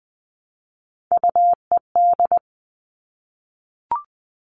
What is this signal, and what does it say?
Morse code, heard as a keyed tone, about 20 wpm, 700 Hz. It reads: UED
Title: All Cut Numbers - 3-Digit 20wpm